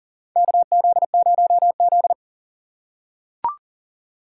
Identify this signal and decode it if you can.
Morse code, heard as a keyed tone, about 37 wpm, 700 Hz. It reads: KZ0Z
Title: USA Call Signs - 6x Character Spacing - 40wpm